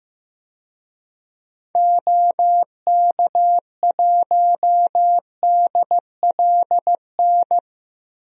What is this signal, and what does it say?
Morse code, heard as a keyed tone, about 15 wpm, 700 Hz. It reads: OK1DLN
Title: DXCC-100 Call Signs - 6 Characters - Repeated 3x - 15wpm